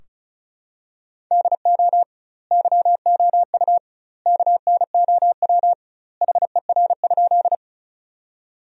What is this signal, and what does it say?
Morse code, heard as a keyed tone, about 34 wpm, 700 Hz. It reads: DO YOU KNOW HER?
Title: 2x Word-Spacing - Sentences from Top 100 Words - 35wpm